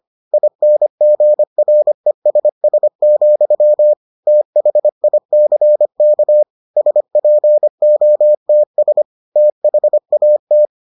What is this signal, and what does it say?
Morse code, heard as a keyed tone, about 25 wpm, 600 Hz. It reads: INGRESS, THICK SPOTS THAT